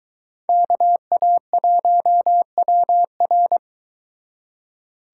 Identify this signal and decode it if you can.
Morse code, heard as a keyed tone, about 23 wpm, 700 Hz. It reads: KA1WR